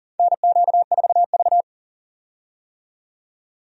Morse code, heard as a keyed tone, about 40 wpm, 700 Hz. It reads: NQ4V
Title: USA Call Signs - 4 Characters - 40wpm